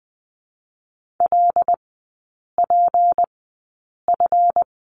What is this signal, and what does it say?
Morse code, heard as a keyed tone, about 20 wpm, spaced as for 8 wpm, 700 Hz. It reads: LPF
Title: 218 Lesson 35e - Number 0 - Three Characters 20wpm